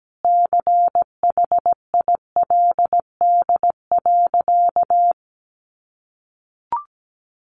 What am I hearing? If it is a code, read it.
Morse code, heard as a keyed tone, about 17 wpm, 700 Hz. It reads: CHILD.
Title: Sentences from Top 800 Words 17wpm